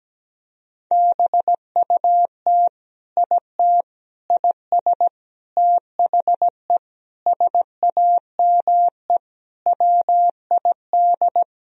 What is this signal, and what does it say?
Morse code, heard as a keyed tone, about 17 wpm, 700 Hz. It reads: BUT IT IS THE SAME WID
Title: Sentences from Top 300 Words and Abbreviations 17wpm